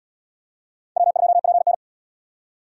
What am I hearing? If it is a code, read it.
Morse code, heard as a keyed tone, about 77 wpm, 700 Hz. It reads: W2PN